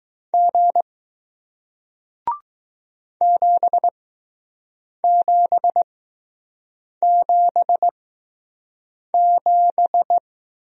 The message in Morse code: G 7777

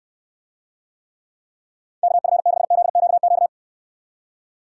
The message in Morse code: KF6ZQQ

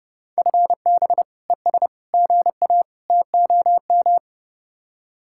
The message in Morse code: FB ES GA TOM

